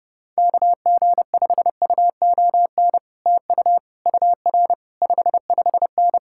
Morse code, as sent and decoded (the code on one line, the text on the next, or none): -.- --. ..... ..- --- -. / - ..- / ..- .-. / ..... ..... -.
KG5UON TU UR 55N